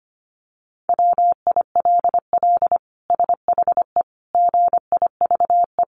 Morse code, heard as a keyed tone, about 25 wpm, 700 Hz. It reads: WILL SHE GIVE
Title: Sentences from Top 1400 Words 25wpm